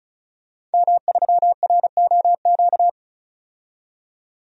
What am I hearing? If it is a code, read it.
Morse code, heard as a keyed tone, about 36 wpm, 700 Hz. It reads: M3ROQ
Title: DXCC-100 Call Signs - 5 Characters - 35wpm